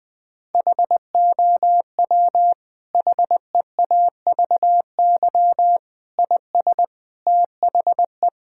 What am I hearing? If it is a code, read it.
Morse code, heard as a keyed tone, about 20 wpm, 700 Hz. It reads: HOW HEAVY IS THE
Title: Sentences from Top 1000 Words - Review Part 6 - 20wpm